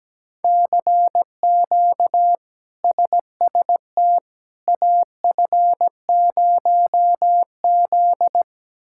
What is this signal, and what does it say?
Morse code, heard as a keyed tone, about 17 wpm, 700 Hz. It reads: CQ SST AF0Z
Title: K1USN SST QSO Practice - 17wpm